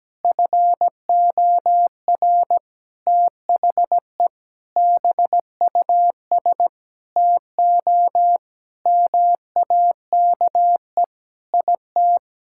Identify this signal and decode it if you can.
Morse code, heard as a keyed tone, about 17 wpm, 700 Hz. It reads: FOR THE BUS TO MAKE IT
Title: Sentences from Top 1000 Words - Review Part 6 - 17wpm